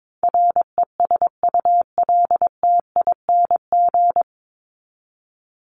Morse code, heard as a keyed tone, about 22 wpm, 700 Hz. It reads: RESULTING
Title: ING Suffix - Top 100 Words with Base Words 22wpm